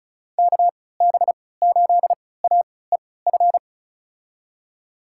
Morse code, heard as a keyed tone, about 36 wpm, spaced as for 20 wpm, 700 Hz. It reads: KB8AEF
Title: USA Call Signs - 3x Character Spacing - 35wpm